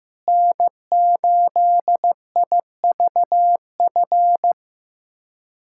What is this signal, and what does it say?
Morse code, heard as a keyed tone, about 15 wpm, 700 Hz. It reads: N8IVF
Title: USA Callsign DE Callsign 15wpm